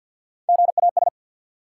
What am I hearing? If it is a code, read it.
Morse code, heard as a keyed tone, about 50 wpm, 700 Hz. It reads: GAS
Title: Sentences from Top 1100 Words - New Word Drill - 50wpm